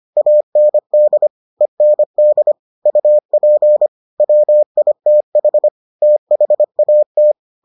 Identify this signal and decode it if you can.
Morse code, heard as a keyed tone, about 25 wpm, 600 Hz. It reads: AND END UP WITH THAT